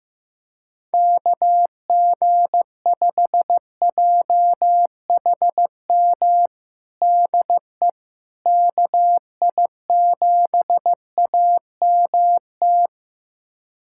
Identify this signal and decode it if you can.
Morse code, heard as a keyed tone, about 15 wpm, 700 Hz. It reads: KG5JHM DE KI7AMT
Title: USA Callsign DE Callsign - 6 Characters - 15wpm